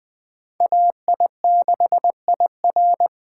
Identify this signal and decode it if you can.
Morse code, heard as a keyed tone, about 20 wpm, 700 Hz. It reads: AI6IR